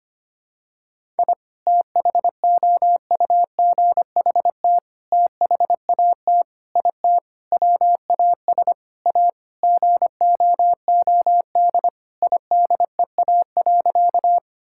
Morse code, heard as a keyed tone, about 25 wpm, 700 Hz. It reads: I THOUGHT THAT IT WAS A GOOD IDEA.
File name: practice / 25wpm / Sentences with 4-Word Phrases - 25wpm